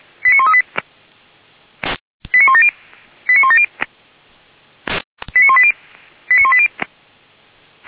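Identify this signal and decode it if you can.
Five-tone selective call (ZVEI), heard as quick runs of five tones, tones of about 80 ms. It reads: 89138 89170 89170 89180 89180
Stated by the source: Five-tone ZVEI selective call received by SDR, from the server of Feuerwehr Orlamünde.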